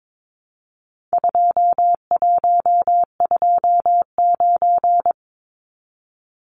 Morse code, heard as a keyed tone, about 22 wpm, 700 Hz. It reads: 2129
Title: Numbers - 4-Digit 22wpm